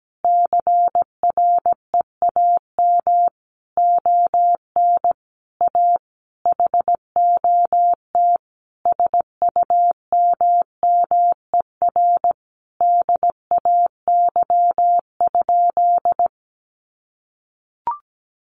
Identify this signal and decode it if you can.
Morse code, heard as a keyed tone, about 17 wpm, 700 Hz. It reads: CREAM ON A HOT SUMMER DAY?
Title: Sentences from Top 1500 Words 17wpm